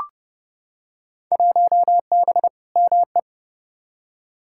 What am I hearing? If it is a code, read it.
Morse code, heard as a keyed tone, about 30 wpm, 700 Hz. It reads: 1B ME
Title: Field Day Exchanges - 30wpm